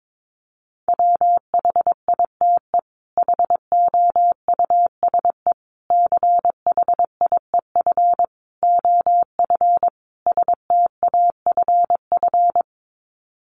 Morse code, heard as a keyed tone, about 22 wpm, 700 Hz. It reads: WHITE HOUSE CHIEF OF STAFF
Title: Sets of 5 Words Encore 22wpm